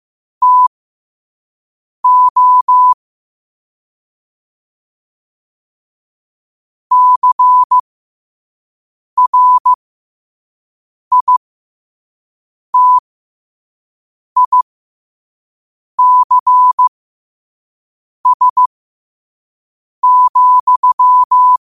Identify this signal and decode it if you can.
Morse code, heard as a keyed tone, about 15 wpm, spaced as for 5 wpm, 1000 Hz. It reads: TO CRITICS,
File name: qotdfiles / 5wpm